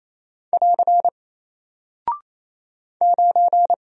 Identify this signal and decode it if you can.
Morse code, heard as a keyed tone, about 28 wpm, 700 Hz. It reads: <AR> 9
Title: Single Character (Letter:Number:Punctuation:Prosign) Alt 28wpm